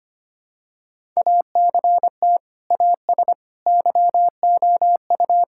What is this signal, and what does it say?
Morse code, heard as a keyed tone, about 25 wpm, 700 Hz. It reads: ACT AS YOU
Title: Sentences from Top 1000 Words - Review Part 5 - 25wpm